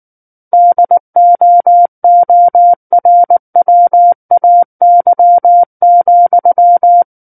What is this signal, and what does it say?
Morse code, heard as a keyed tone, about 19 wpm, 700 Hz. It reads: DOORWAY,
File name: Book One / 10-Chapter 10 - 19 WPM